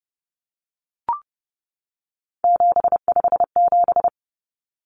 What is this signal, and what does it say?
Morse code, heard as a keyed tone, about 30 wpm, 700 Hz. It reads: 757
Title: Common Cut Numbers - 3-Digit 30wpm